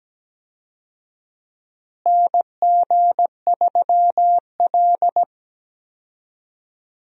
Morse code, heard as a keyed tone, about 17 wpm, 700 Hz. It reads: NG3L
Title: USA Call Signs - 4 Characters - Repeated 2x - 17wpm